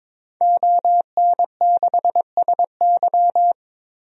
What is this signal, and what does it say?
Morse code, heard as a keyed tone, about 22 wpm, 700 Hz. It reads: ON6SY
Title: DXCC-100 Call Signs - 5 Characters - 22wpm